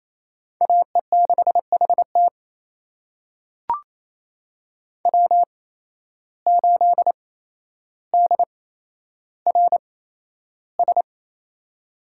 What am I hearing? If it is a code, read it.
Morse code, heard as a keyed tone, about 28 wpm, spaced as for 12 wpm, 700 Hz. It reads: AE6HT W8DRS
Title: USA Call Signs - 5 Characters - 8x Character Spacing - 28wpm